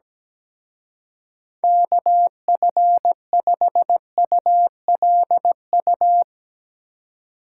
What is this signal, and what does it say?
Morse code, heard as a keyed tone, about 18 wpm, 700 Hz. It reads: KF5ULU
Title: Speed Racing - USA 2x3 Call Signs 17wpm